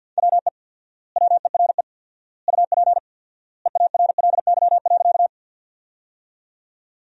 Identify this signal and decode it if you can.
Morse code, heard as a keyed tone, about 52 wpm, 700 Hz. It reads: WE WERE UP EARLY.
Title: 4x Word-Spacing - Sentences from Top 400 Words - 50wpm